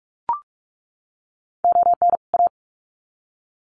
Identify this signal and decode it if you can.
Morse code, heard as a keyed tone, about 44 wpm, 700 Hz. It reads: ON A
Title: Sets of 2 Words 45wpm